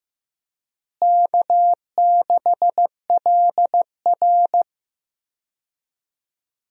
Morse code, heard as a keyed tone, about 15 wpm, 700 Hz. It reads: K6LR
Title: USA Call Signs - 4 Characters - Repeated 3x - 15wpm